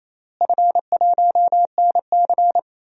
Morse code, heard as a keyed tone, about 27 wpm, 700 Hz. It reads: F1NC
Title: DXCC-100 Call Signs - 4 Characters - Repeated 2x - 28wpm